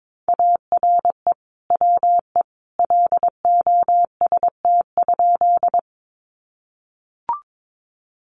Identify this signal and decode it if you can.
Morse code, heard as a keyed tone, about 22 wpm, 700 Hz. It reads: ARE WE LOST?
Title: Sentences from Top 1400 Words 22wpm